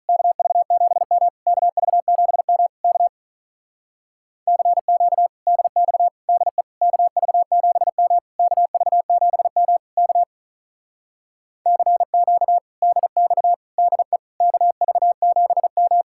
Morse code, heard as a keyed tone, about 44 wpm, 700 Hz. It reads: KV7M KV7M K CQ DX DE KV7M KV7M K CQ DX DE KV7M